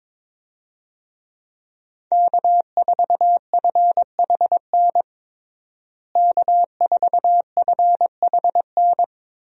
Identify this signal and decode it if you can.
Morse code, heard as a keyed tone, about 22 wpm, 700 Hz. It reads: K4FHN K4FHN
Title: USA Call Signs - 5 Characters - Repeated 2x - 22wpm